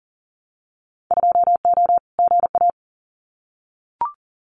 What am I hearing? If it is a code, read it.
Morse code, heard as a keyed tone, about 40 wpm, 700 Hz. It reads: 2O GA
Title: Winter Field Day Exchanges Encore 40wpm